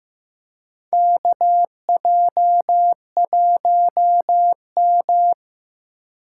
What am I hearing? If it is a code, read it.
Morse code, heard as a keyed tone, about 15 wpm, 700 Hz. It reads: KJ1M
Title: USA Call Signs - 4 Characters - 6x Character Spacing - 15wpm